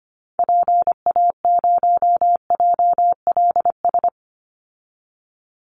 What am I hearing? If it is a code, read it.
Morse code, heard as a keyed tone, about 25 wpm, 700 Hz. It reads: PA0JLS
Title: Call Signs Repeated 3x - 25wpm